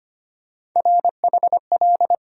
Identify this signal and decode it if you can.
Morse code, heard as a keyed tone, about 25 wpm, 700 Hz. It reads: RHL
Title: USA Call Sign Postfixes - 3 Letters 25wpm